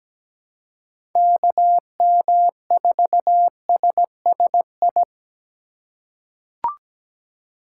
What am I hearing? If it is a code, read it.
Morse code, heard as a keyed tone, about 17 wpm, 700 Hz. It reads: KM4SSI
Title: USA Call Signs - 6 Characters - 2x Character Spacing - 17wpm